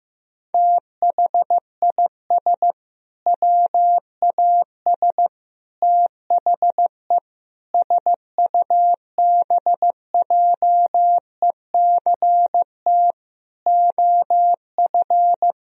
Morse code, heard as a keyed tone, about 15 wpm, 700 Hz. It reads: THIS WAS THE SUBJECT OF
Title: Sentences from Top 1000 Words - Review Part 1 - 15wpm